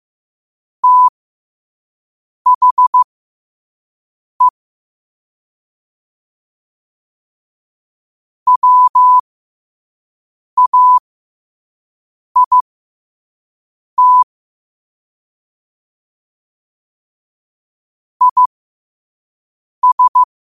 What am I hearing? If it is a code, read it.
Morse code, heard as a keyed tone, about 15 wpm, spaced as for 5 wpm, 1000 Hz. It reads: THE WAIT IS
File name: qotdfiles / 5wpm